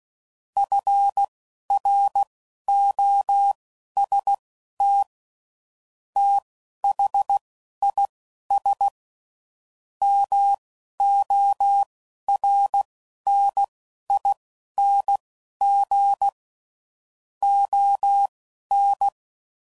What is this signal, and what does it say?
Morse code, heard as a keyed tone, about 16 wpm, spaced as for 12 wpm, 800 Hz. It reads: FROST THIS MORNING ON
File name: CW Code Groups @10WPM / Weather Sets / Weather 4 24